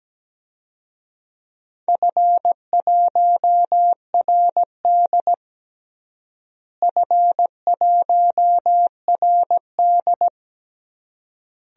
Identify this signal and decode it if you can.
Morse code, heard as a keyed tone, about 17 wpm, 700 Hz. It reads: F1RD F1RD
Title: DXCC-100 Call Signs - 4 Characters - Repeated 3x - 17wpm